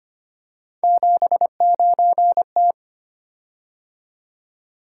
Morse code, heard as a keyed tone, about 25 wpm, 700 Hz. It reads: 79T
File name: Three Letter-Number Combinations - 25wpm